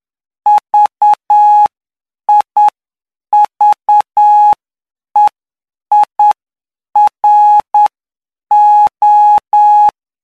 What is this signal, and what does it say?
Morse code, heard as a keyed tone, about 9 wpm, spaced as for 7 wpm, 840 Hz. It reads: VIVEIRO
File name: cw / py2lcd / 11-PY2LCD-Li��o 20